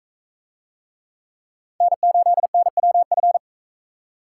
Morse code, heard as a keyed tone, about 40 wpm, 700 Hz. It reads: N8NWF